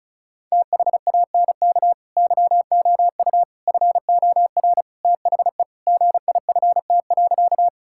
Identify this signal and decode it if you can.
Morse code, heard as a keyed tone, about 36 wpm, 700 Hz. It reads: THANK YOU FOR THE GIFT.
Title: Sentences from Top 1000 Words - Review Part 6 - 35wpm